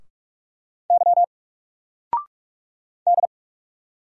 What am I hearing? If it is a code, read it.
Morse code, heard as a keyed tone, about 44 wpm, 700 Hz. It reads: Y D
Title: Single Letter-Number 45wpm